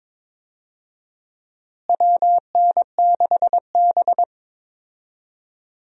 Morse code, heard as a keyed tone, about 22 wpm, 700 Hz. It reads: WN6B
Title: USA Call Signs - 4 Characters - Repeated 2x - 22wpm